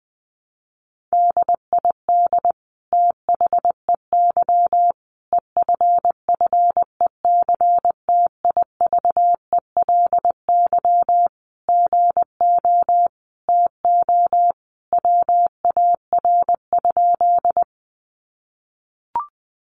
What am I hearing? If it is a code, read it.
Morse code, heard as a keyed tone, about 20 wpm, 700 Hz. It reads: DID THEY EFFECTIVELY GO TO WAR?
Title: Sentences from Top 1400 Words 20wpm